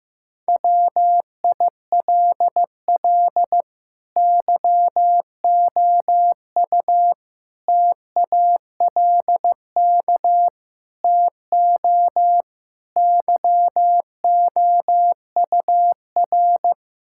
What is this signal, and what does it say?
Morse code, heard as a keyed tone, about 15 wpm, 700 Hz. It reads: WILL YOU TALK TO YOUR
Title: Sentences from Top 1600 Words 15wpm